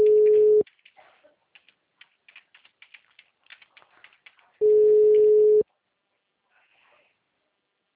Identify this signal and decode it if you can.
Ringback tone (Europe, 425 Hz).